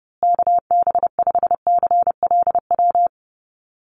Morse code, heard as a keyed tone, about 30 wpm, 700 Hz. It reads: KB5CLW
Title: USA Call Signs Repeated 2x 30wpm